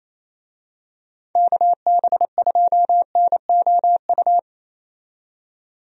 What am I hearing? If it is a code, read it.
Morse code, heard as a keyed tone, about 28 wpm, 700 Hz. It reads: KB2NOU
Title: USA Call Signs - 6 Characters - 28wpm